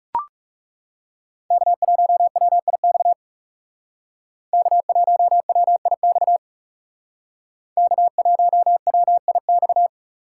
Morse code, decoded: K1WIX K1WIX K1WIX

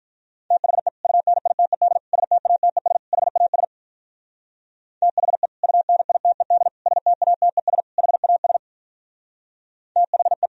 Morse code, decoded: THE UNITED STATES HAS THE UNITED STATES HAS THE

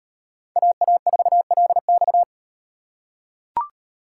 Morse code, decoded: AA4LX

